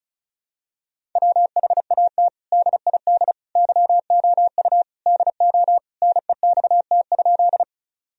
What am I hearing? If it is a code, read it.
Morse code, heard as a keyed tone, about 36 wpm, 700 Hz. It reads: WHAT DID YOU DO NEXT?
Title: Sentences from Top 1000 Words - Review Part 1 - 35wpm